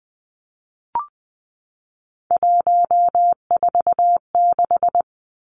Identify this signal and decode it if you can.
Morse code, heard as a keyed tone, about 20 wpm, 700 Hz. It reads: 146
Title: Numbers - 3-Digit 20wpm